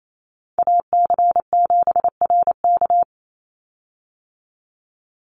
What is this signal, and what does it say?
Morse code, heard as a keyed tone, about 28 wpm, 700 Hz. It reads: AC7RK
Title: USA Call Signs 28wpm